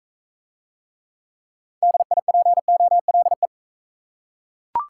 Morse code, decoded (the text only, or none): DIPOLE